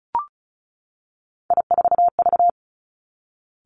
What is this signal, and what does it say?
Morse code, heard as a keyed tone, about 36 wpm, 700 Hz. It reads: I4V
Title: Three Challenging Letter-Number Combinations 35wpm